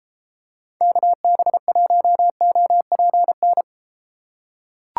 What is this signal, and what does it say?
Morse code, heard as a keyed tone, about 34 wpm, 700 Hz. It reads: KB1OPN